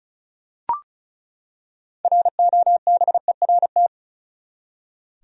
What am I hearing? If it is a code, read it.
Morse code, heard as a keyed tone, about 36 wpm, 700 Hz. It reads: ROBERT
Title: USA Common Names - 35wpm